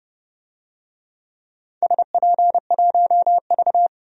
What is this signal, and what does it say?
Morse code, heard as a keyed tone, about 30 wpm, 700 Hz. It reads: SP1V
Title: DXCC-100 Call Signs - 4 Characters - 30wpm